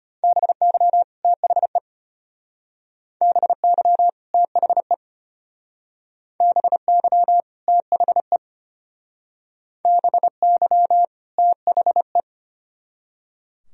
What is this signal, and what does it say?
Morse code, heard as a keyed tone, about 31 wpm, 700 Hz. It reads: BY THE BY THE BY THE BY THE
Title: Speed-Racing - Sets of 2 Words 25wpm